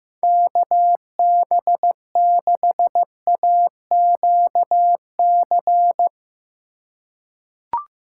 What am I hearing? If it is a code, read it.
Morse code, heard as a keyed tone, about 15 wpm, 700 Hz. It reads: KB6AQC
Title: USA Call Signs - 2x Character Spacing - 15wpm